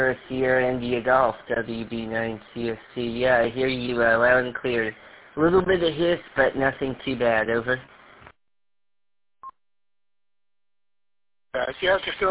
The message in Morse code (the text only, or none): none